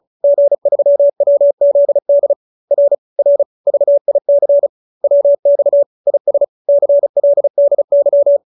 Morse code, heard as a keyed tone, about 35 wpm, 570 Hz. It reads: G3WZD R R VIC WX IS CLDY
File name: QSO310 35